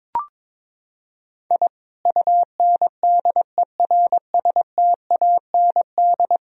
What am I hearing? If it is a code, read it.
Morse code, heard as a keyed tone, about 22 wpm, 700 Hz. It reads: I UNDERSTAND
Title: Sentences from Top 1000 Words - Review Part 1 - 22wpm